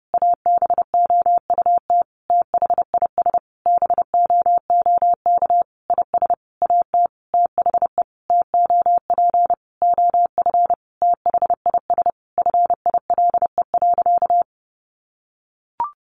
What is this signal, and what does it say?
Morse code, heard as a keyed tone, about 30 wpm, 700 Hz. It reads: ABOUT THIS BOOK IS AT THE TOP OF THIS FILE.
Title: Sentences from Top 1200 Words 30wpm